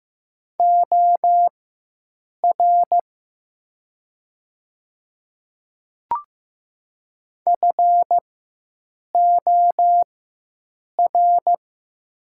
Morse code, heard as a keyed tone, about 15 wpm, spaced as for 7 wpm, 690 Hz. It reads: OR FOR
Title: 4x Character Spacing - Top 500 Words - 15wpm